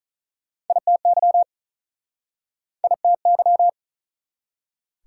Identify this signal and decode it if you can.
Morse code, heard as a keyed tone, about 47 wpm, 700 Hz. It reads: ITY ITY